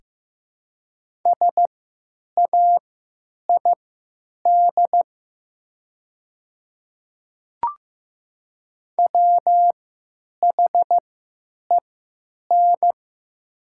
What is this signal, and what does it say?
Morse code, heard as a keyed tone, about 15 wpm, spaced as for 9 wpm, 700 Hz. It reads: SAID WHEN
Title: 3x Character Spacing - Top 100 Words - 15wpm